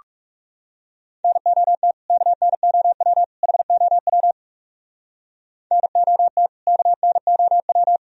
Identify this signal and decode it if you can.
Morse code, heard as a keyed tone, about 37 wpm, 700 Hz. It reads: NOT KNOW HOW NOT KNOW
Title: Speed-Racing - Sets of 3 Words 30wpm